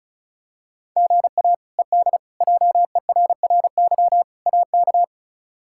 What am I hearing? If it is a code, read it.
Morse code, heard as a keyed tone, about 36 wpm, 700 Hz. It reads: GA ED JERRY AK
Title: K1USN SST QSO Practice - 35wpm